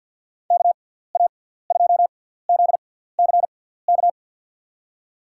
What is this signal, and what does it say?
Morse code, heard as a keyed tone, about 51 wpm, spaced as for 17 wpm, 700 Hz. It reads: KA2ZCK
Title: USA Call Signs - 6 Characters - 6x Character Spacing - 50wpm